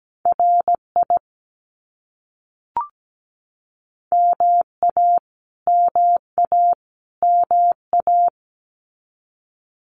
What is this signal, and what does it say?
Morse code, heard as a keyed tone, about 17 wpm, 700 Hz. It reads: RI MA MA MA
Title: US States Repeated 3x 17wpm